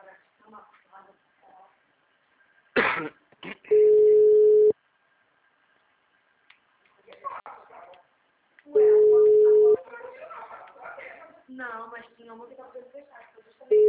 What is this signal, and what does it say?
Ringback tone (Europe, 425 Hz), with speech.